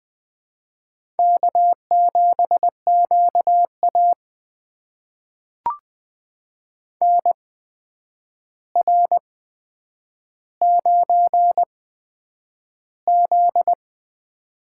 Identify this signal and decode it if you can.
Morse code, heard as a keyed tone, about 20 wpm, spaced as for 9 wpm, 700 Hz. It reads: K7QA NR9Z
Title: USA Call Signs - 4 Characters - 8x Character Spacing - 20wpm